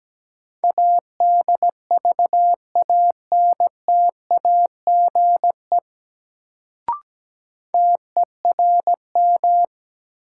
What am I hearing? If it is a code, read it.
Morse code, heard as a keyed tone, about 17 wpm, 700 Hz. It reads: ADVANTAGE TERM
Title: Sentences from Top 900 Words - New Word Drill - 17wpm